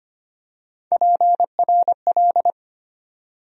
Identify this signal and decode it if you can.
Morse code, heard as a keyed tone, about 25 wpm, 700 Hz. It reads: PRL
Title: USA Call Sign Postfixes - 3 Letters 25wpm